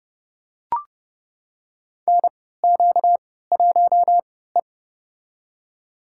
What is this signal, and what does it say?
Morse code, heard as a keyed tone, about 32 wpm, spaced as for 17 wpm, 700 Hz. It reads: NQ1E
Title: USA Call Signs - 3x Character Spacing - 30wpm